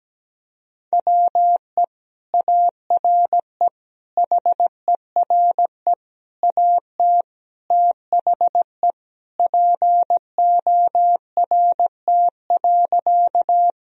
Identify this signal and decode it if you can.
Morse code, heard as a keyed tone, about 17 wpm, 700 Hz. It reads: WE ARE HERE AT THE PORT.
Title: Sentences from Top 1000 Words - Review Part 1 - 17wpm